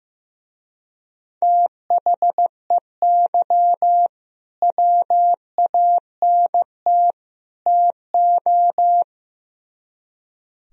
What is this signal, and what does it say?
Morse code, heard as a keyed tone, about 16 wpm, 700 Hz. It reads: THEY WANT TO
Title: Speed-Racing - Sets of 3 Words 15wpm